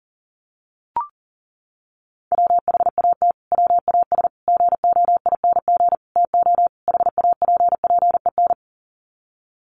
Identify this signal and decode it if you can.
Morse code, heard as a keyed tone, about 37 wpm, 700 Hz. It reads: WHAT WAS GOING TO HAPPEN
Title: Sets of 5 Words Encore 40wpm